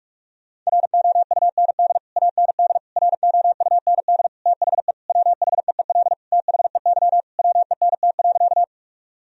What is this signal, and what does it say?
Morse code, heard as a keyed tone, about 46 wpm, 700 Hz. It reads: ROUND AND ROUND THE WHEEL THEY WENT.
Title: Sentences from Top 1000 Words - Review Part 3 - 45wpm